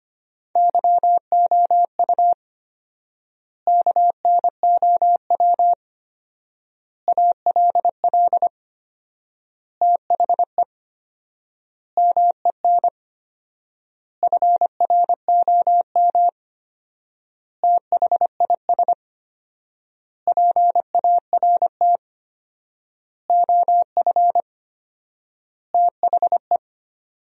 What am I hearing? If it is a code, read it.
Morse code, heard as a keyed tone, about 25 wpm, 700 Hz. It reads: YOU KNOW ALL THE MEN FROM THIS PART OF THE
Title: 4x Word-Spacing - Sentences from Top 300 Words - 25wpm